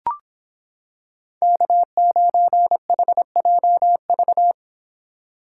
Morse code, heard as a keyed tone, about 26 wpm, 700 Hz. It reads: K9HJV